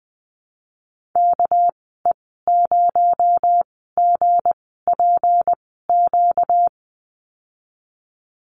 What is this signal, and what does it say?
Morse code, heard as a keyed tone, about 20 wpm, spaced as for 14 wpm, 700 Hz. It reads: KE0GPQ
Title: 245 Lesson 39g - Letter z - Call Signs 20wpm